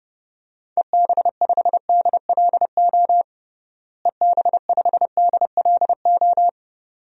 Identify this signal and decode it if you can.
Morse code, heard as a keyed tone, about 32 wpm, 700 Hz. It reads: EB5DLO EB5DLO
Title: DXCC-100 Call Signs - 6 Characters - Repeated 2x - 30wpm